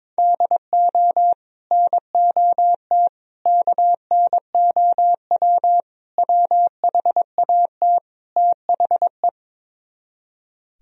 Morse code, heard as a keyed tone, about 23 wpm, 700 Hz. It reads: DO NOT KNOW WHAT THE
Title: Speed-Racing - Sets of 5 Words 22wpm